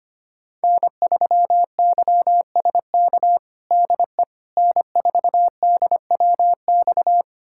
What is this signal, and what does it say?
Morse code, heard as a keyed tone, about 25 wpm, 700 Hz. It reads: N3YSK DE N4DWX
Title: USA Callsign DE Callsign - 5 Characters - 25wpm